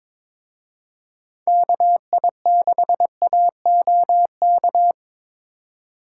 Morse code, heard as a keyed tone, about 25 wpm, 700 Hz. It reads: KI6AOK
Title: Speed Racing - USA 2x3 Call Signs 22wpm